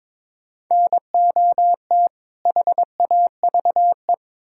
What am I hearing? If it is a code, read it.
Morse code, heard as a keyed tone, about 22 wpm, 700 Hz. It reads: NOT HAVE
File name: Build A Phrase - 2-Word Phrases - 22wpm